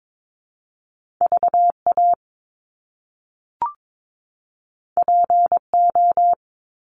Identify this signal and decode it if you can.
Morse code, heard as a keyed tone, about 22 wpm, 700 Hz. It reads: VA PO
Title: Top Two Letter Combinations in Words 22wpm